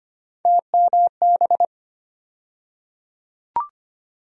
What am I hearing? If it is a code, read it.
Morse code, heard as a keyed tone, about 25 wpm, 700 Hz. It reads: TMB
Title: USA Call Sign Postfixes - 3 Letters 25wpm